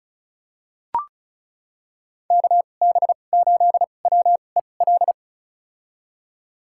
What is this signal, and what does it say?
Morse code, heard as a keyed tone, about 36 wpm, spaced as for 25 wpm, 700 Hz. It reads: KB8WEL
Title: USA Call Signs - 6 Characters - 2x Character Spacing - 35wpm